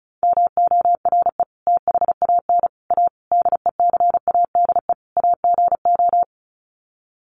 Morse code, heard as a keyed tone, about 34 wpm, 700 Hz. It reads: MORE THAN A DECADE AGO
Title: Sets of 5 Words Encore2 35wpm